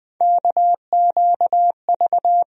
Morse code, heard as a keyed tone, about 20 wpm, 700 Hz. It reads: KQV
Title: USA Call Sign Postfixes - 3 Letters 20wpm